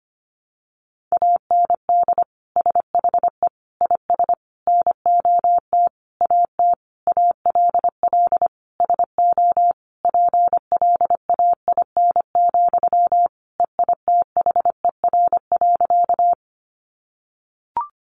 Morse code, heard as a keyed tone, about 25 wpm, 700 Hz. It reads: AND SHE IS NOT AT ALL SO PLAIN, EITHER.
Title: Sentences from Top 600 Words 25wpm